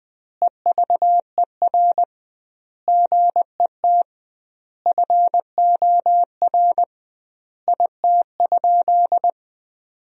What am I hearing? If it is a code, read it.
Morse code, heard as a keyed tone, about 19 wpm, 700 Hz. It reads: EVER GET FOR IT?
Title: 2x Word-Spacing - Sentences from Top 400 Words - 20wpm